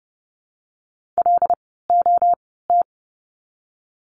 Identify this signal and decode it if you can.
Morse code, heard as a keyed tone, about 30 wpm, spaced as for 17 wpm, 700 Hz. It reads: LOT
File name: course / 30:10wpm / 108 Lesson 18f - Letter m - Words 30wpm